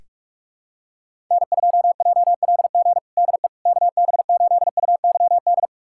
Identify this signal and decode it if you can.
Morse code, heard as a keyed tone, about 46 wpm, 690 Hz. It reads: N2JLG DE KB8UYD
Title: USA Callsign DE Callsign 45wpm